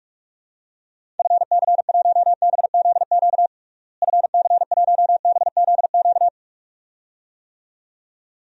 Morse code, heard as a keyed tone, about 46 wpm, 700 Hz. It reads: FC1BZQ FC1BZQ
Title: DXCC-100 Call Signs - 6 Characters - Repeated 2x - 45wpm